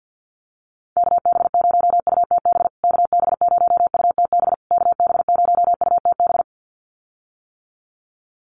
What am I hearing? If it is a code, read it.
Morse code, heard as a keyed tone, about 50 wpm, 700 Hz. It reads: KB0UTB KB0UTB KB0UTB
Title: USA Call Signs Repeated 3x 50wpm